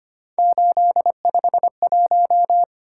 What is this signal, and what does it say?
Morse code, heard as a keyed tone, about 25 wpm, 700 Hz. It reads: 851
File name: Three Letter-Number Combinations - 25wpm